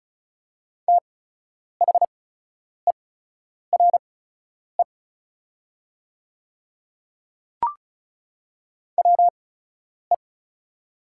Morse code, heard as a keyed tone, about 36 wpm, spaced as for 10 wpm, 700 Hz. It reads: THERE WE
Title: 8x Character Spacing - Top 100 Words - 35wpm